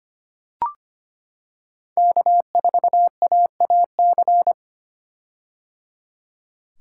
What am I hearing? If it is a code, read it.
Morse code, heard as a keyed tone, about 25 wpm, 700 Hz. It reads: K4AAC
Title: K1USN SST QSO Practice - 25wpm